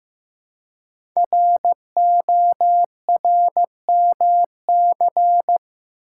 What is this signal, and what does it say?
Morse code, heard as a keyed tone, about 15 wpm, 700 Hz. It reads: RORMC
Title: Code Groups - 5 Letters - 15wpm